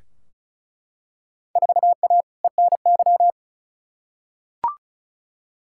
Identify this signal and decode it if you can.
Morse code, heard as a keyed tone, about 36 wpm, 700 Hz. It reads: 4A ENY
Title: Field Day Exchanges Encore - 35wpm